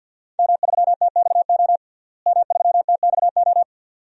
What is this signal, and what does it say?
Morse code, heard as a keyed tone, about 53 wpm, 700 Hz. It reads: M3TXO M3TXO
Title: DXCC-100 Call Signs - 5 Characters - Repeated 2x - 50wpm